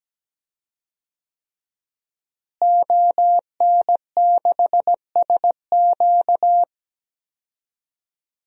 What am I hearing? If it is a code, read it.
Morse code, heard as a keyed tone, about 17 wpm, 700 Hz. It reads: ON6SQ
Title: DXCC-100 Call Signs - 5 Characters - Repeated 3x - 17wpm